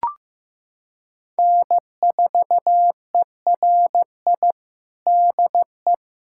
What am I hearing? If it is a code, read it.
Morse code, heard as a keyed tone, about 16 wpm, 700 Hz. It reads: N4ERI DE
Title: USA Callsign DE Callsign - 5 Characters - 15wpm